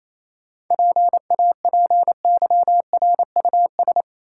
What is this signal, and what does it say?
Morse code, heard as a keyed tone, about 27 wpm, 700 Hz. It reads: PAPYRUS
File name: Easy Words - Speed-Racing with Hints - 22wpm